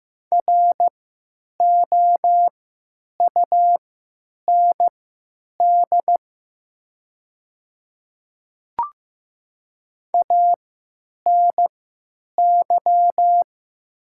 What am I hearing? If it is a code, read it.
Morse code, heard as a keyed tone, about 15 wpm, spaced as for 9 wpm, 700 Hz. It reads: ROUND ANY
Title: 3x Character Spacing - Top 300 Words - 15wpm